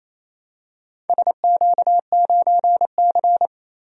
Morse code, heard as a keyed tone, about 27 wpm, 700 Hz. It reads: SQ9C
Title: DXCC-100 Call Signs - 4 Characters - 28wpm